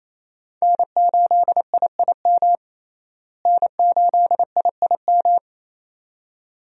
Morse code, heard as a keyed tone, about 27 wpm, 700 Hz. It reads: N8IIM N8IIM
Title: USA Call Signs - 5 Characters - Repeated 2x - 28wpm